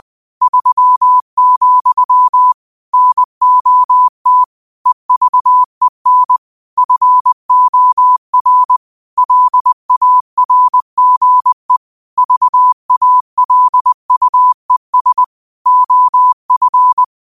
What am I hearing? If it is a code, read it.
Morse code, heard as a keyed tone, about 20 wpm, 1000 Hz. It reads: 3, NOT EVEN FOR LARGE VALUES OF